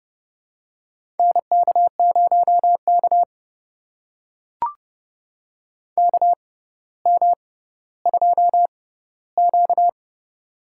Morse code, heard as a keyed tone, about 32 wpm, spaced as for 16 wpm, 690 Hz. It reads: NK0K KM2Q
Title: USA Call Signs - 4 Characters - 6x Character Spacing - 30wpm